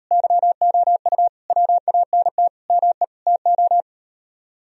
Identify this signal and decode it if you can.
Morse code, heard as a keyed tone, about 38 wpm, 700 Hz. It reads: YOU WANT ME TO